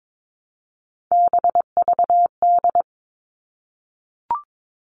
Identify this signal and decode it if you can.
Morse code, heard as a keyed tone, about 22 wpm, 700 Hz. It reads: BVD
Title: All Cut Numbers - 3-Digit 22wpm